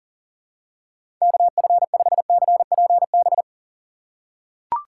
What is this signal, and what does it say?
Morse code, heard as a keyed tone, about 40 wpm, 700 Hz. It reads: KF5CPB
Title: USA Call Signs - 6 Characters - 6x Character Spacing - 40wpm